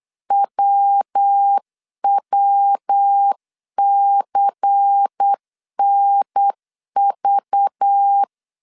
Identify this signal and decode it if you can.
Morse code, heard as a keyed tone, about 8 wpm, 800 Hz. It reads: WWCNV
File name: CW / DET / 1 6 1500C S C 8wpm 5219